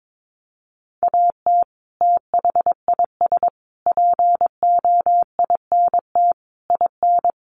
Morse code, heard as a keyed tone, about 22 wpm, 700 Hz. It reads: AT THIS POINT IN
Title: Sets of 5 Words Encore 22wpm